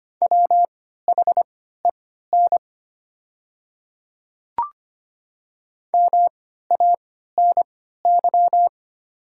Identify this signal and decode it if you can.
Morse code, heard as a keyed tone, about 25 wpm, spaced as for 14 wpm, 700 Hz. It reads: WHEN MANY
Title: 3x Character Spacing - Top 300 Words - 25wpm